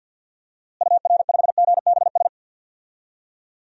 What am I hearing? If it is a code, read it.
Morse code, heard as a keyed tone, about 50 wpm, 700 Hz. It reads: UR5GBS
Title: DXCC-100 Call Signs - 6 Characters - Repeated 2x - 50wpm